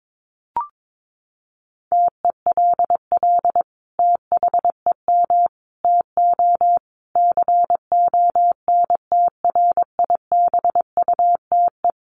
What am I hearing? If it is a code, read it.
Morse code, heard as a keyed tone, about 22 wpm, 700 Hz. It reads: TELL THEM TO CONTRIBUTE
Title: Sentences from Top 1400 Words 22wpm